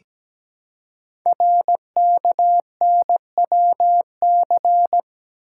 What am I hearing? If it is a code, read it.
Morse code, heard as a keyed tone, about 17 wpm, 700 Hz. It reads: RKNWC
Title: Code Groups - 5 Letters - 17wpm